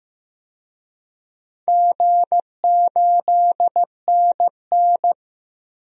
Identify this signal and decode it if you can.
Morse code, heard as a keyed tone, about 15 wpm, 700 Hz. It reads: G8NN
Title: DXCC-100 Call Signs - 4 Characters - 15wpm